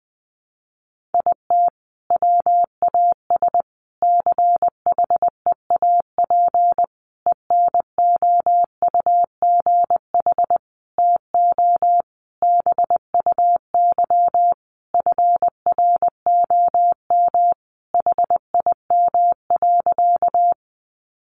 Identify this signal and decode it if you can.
Morse code, heard as a keyed tone, about 19 wpm, 700 Hz. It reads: IT WAS CHEAP ENOUGH TO BUY FROM HIM.
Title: Sentences from Top 1200 Words 20wpm